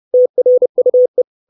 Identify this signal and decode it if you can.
Morse code, heard as a keyed tone, about 30 wpm, 500 Hz. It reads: TRUE
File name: 632 Words 30WPM